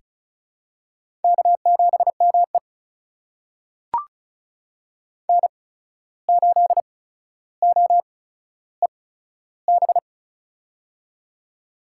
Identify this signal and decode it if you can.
Morse code, heard as a keyed tone, about 36 wpm, spaced as for 14 wpm, 700 Hz. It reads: K7ME N8OEB
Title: USA Call Signs - 8x Character Spacing - 35wpm